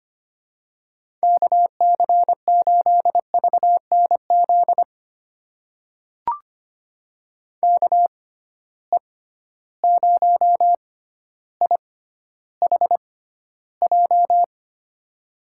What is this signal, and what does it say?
Morse code, heard as a keyed tone, about 25 wpm, spaced as for 13 wpm, 700 Hz. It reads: KC8VNZ KE0IHJ
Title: USA Call Signs - 6 Characters - 6x Character Spacing - 25wpm